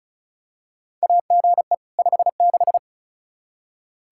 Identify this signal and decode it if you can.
Morse code, heard as a keyed tone, about 36 wpm, 700 Hz. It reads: AGE 56
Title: K1USN SST QSO Practice 35wpm